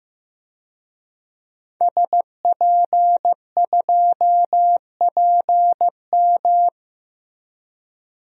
Morse code, heard as a keyed tone, about 16 wpm, 700 Hz. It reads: SP2PM
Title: DXCC-100 Call Signs - 5 Characters - Repeated 3x - 15wpm